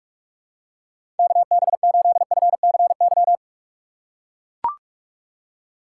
Morse code, heard as a keyed tone, about 44 wpm, 700 Hz. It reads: KB8FCY
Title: USA Call Signs - 6 Characters - 8x Character Spacing - 45wpm